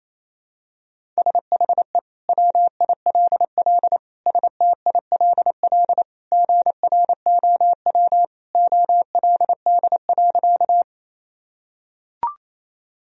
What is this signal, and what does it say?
Morse code, heard as a keyed tone, about 28 wpm, 700 Hz. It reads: SHE WILL STILL GROW OLD.
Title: Sentences from Top 1000 Words - Review Part 1 - 28wpm